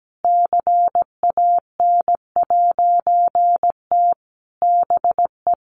Morse code, heard as a keyed tone, about 17 wpm, 700 Hz. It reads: CAN'T BE
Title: A Sound Of Thunder 17wpm